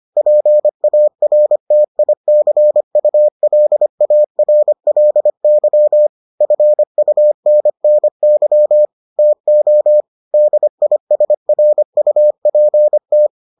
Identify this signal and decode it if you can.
Morse code, heard as a keyed tone, about 25 wpm, 600 Hz. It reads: PARTICULARLY FUNNY TO DISRUPT